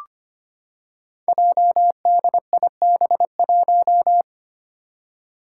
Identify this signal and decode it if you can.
Morse code, heard as a keyed tone, about 25 wpm, 700 Hz. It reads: JDIB1
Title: Five Challenging Letter-Number Combinations 25wpm